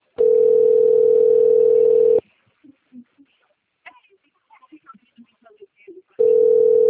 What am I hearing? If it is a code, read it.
Ringback tone (North America), 440 + 480 Hz, with speech.
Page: Jaguariuna - SP